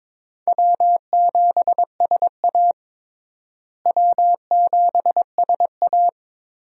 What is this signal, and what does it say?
Morse code, heard as a keyed tone, about 22 wpm, 700 Hz. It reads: W7SA W7SA
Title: USA Call Signs - 4 Characters - Repeated 2x - 22wpm